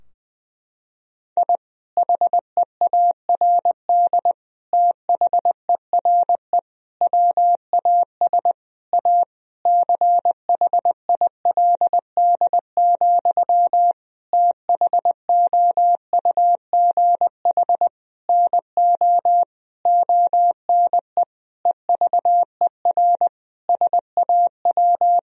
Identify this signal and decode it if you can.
Morse code, heard as a keyed tone, about 20 wpm, 700 Hz. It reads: I HEARD THERE WAS A CHILD, THOUGH NO ONE EVER SAW
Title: Sentences from Top 1000 Words - Review Part 2 - 20wpm